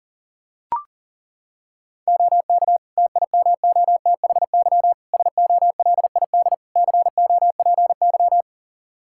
Morse code, heard as a keyed tone, about 40 wpm, 700 Hz. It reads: OK TIMOTHY SOLID COPY
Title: K1USN SST QSO Practice 40wpm